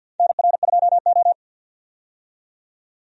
Morse code, heard as a keyed tone, about 49 wpm, 700 Hz. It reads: NR2O
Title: USA Call Signs - 4 Characters - 50wpm